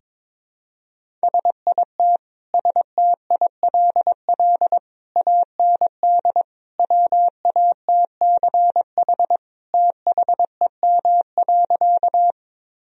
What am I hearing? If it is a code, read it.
Morse code, heard as a keyed tone, about 22 wpm, 700 Hz. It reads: SIT STILL AND WATCH THEM.
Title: Sentences from Top 1000 Words - Review Part 1 - 22wpm